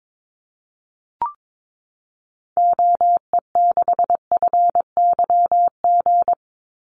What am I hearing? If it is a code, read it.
Morse code, heard as a keyed tone, about 22 wpm, 700 Hz. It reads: OE6FYG
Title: Call Signs Repeated 3x - 22wpm